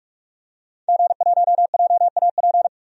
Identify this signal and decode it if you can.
Morse code, heard as a keyed tone, about 44 wpm, 700 Hz. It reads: G1JAP
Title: DXCC-100 Call Signs - 5 Characters - 45wpm